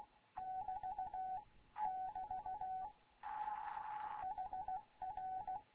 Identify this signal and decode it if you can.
Morse code, heard as a keyed tone, about 16 wpm, 750 Hz. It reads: <BT> <BT> HR
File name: Morze-HA2MN / hangp / 2 11 GYAKORLAT